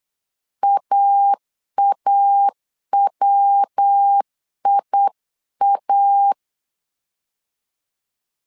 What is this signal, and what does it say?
Morse code, heard as a keyed tone, about 8 wpm, 800 Hz. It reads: AAWIA
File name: CW / DET / 1 3 1500C S C 8wpm 4214